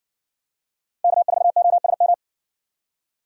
Morse code, heard as a keyed tone, about 60 wpm, 700 Hz. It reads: K4OSM